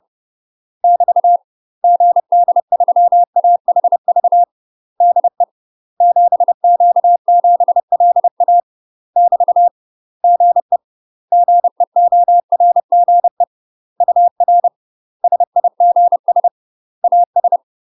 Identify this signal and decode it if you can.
Morse code, heard as a keyed tone, about 30 wpm, 700 Hz. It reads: <BT> GD3AHV DE 7Q7LA <BT> GE GEORGE UR SIGS AS
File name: UK MASTER QSOs TRAINING / 30WPM / UK MASTER QSOs 0024